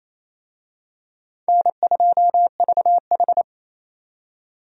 Morse code, heard as a keyed tone, about 28 wpm, 700 Hz. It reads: N2VH
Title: USA Call Signs - 4 Characters - 28wpm